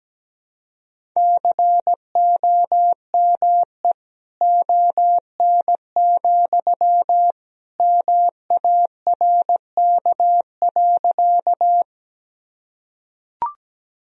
Morse code, heard as a keyed tone, about 17 wpm, 700 Hz. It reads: COME ON, MARK.
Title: Sentences from Top 1000 Words - Review Part 1 - 17wpm